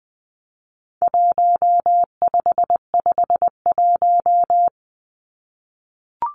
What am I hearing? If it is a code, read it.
Morse code, heard as a keyed tone, about 20 wpm, 700 Hz. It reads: 1551
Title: Numbers - Contest Style - 4-Digit 20wpm